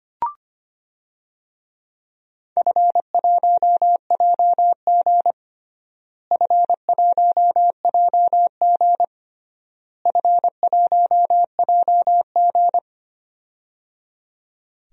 Morse code, heard as a keyed tone, about 25 wpm, 700 Hz. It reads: F1JG F1JG F1JG
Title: DXCC-100 Call Signs - 4 Characters - Repeated 3x - 25wpm